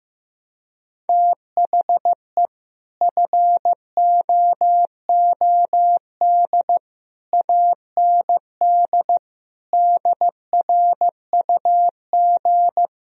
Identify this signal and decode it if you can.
Morse code, heard as a keyed tone, about 15 wpm, 700 Hz. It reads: THE FOOD AND DRUG
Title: Speed-Racing - Sets of 5 Words 15wpm